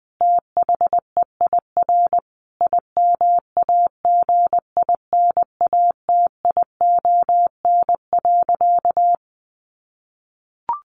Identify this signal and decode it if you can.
Morse code, heard as a keyed tone, about 20 wpm, 700 Hz. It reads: THEIR IMAGINATION.
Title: Sentences from Top 1300 Words 20wpm